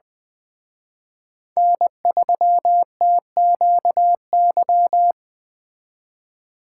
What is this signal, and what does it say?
Morse code, heard as a keyed tone, about 24 wpm, 700 Hz. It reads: N3TQY
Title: Speed Racing - USA 1x3 Call Signs 20wpm